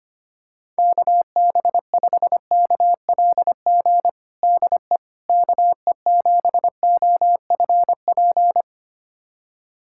Morse code, heard as a keyed tone, about 25 wpm, 700 Hz. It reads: KB5KLG DE KE7OFP
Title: USA Callsign DE Callsign 25wpm